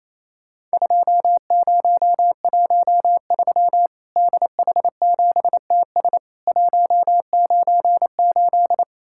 Morse code, heard as a keyed tone, about 28 wpm, 700 Hz. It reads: 2013 DH7TS 198
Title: Worked All Europe DX Contest - QTC Exchange - 28wpm